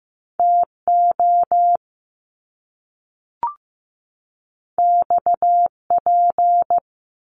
Morse code, heard as a keyed tone, about 15 wpm, 700 Hz. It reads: TO XP
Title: Top Two Letter Combinations in Words 15wpm